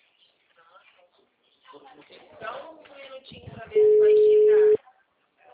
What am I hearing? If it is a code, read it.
Ringback tone (Europe, 425 Hz), with speech.